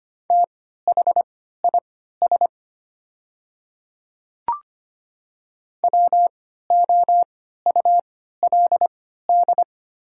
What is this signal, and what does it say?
Morse code, heard as a keyed tone, about 25 wpm, spaced as for 14 wpm, 700 Hz. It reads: THIS WOULD
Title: 3x Character Spacing - Top 100 Words - 25wpm